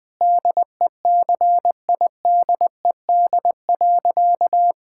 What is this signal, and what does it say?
Morse code, heard as a keyed tone, about 20 wpm, 700 Hz. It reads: DECIDED.
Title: Sentences with 3-Word Phrases - 20wpm